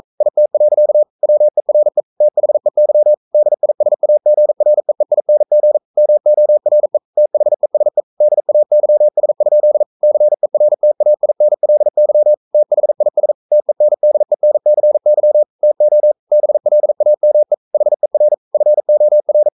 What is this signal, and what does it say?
Morse code, heard as a keyed tone, about 42 wpm, 600 Hz. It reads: IT. WERE THEY DISAGREEING MORE THESE DAYS? CERTAINLY THIS TENDENCY TO BLAME HER FOR
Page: Romanen Never af Ken Follet på engelsk.